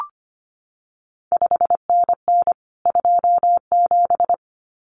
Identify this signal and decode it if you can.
Morse code, heard as a keyed tone, about 25 wpm, 700 Hz. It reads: 5NN 27
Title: CQ WW DX Contest Exchange Encore - 25wpm